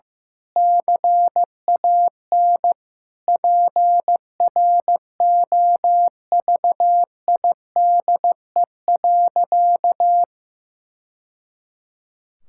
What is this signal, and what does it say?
Morse code, heard as a keyed tone, about 16 wpm, 700 Hz. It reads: CAN PROVIDE.
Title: Sentences from Top 1000 Words - Review Part 6 - 15wpm